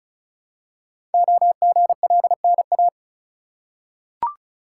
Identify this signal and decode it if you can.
Morse code, heard as a keyed tone, about 36 wpm, 700 Hz. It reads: OGLNA
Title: Code Groups - 5 Letters - 35wpm